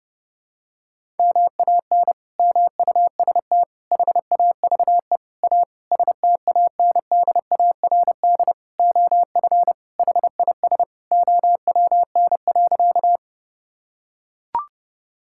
Morse code, heard as a keyed tone, about 32 wpm, 700 Hz. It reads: MAN MUST HAVE A STANDARD OF HIS OWN.
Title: Sentences from Top 1000 Words - Review Part 6 - 30wpm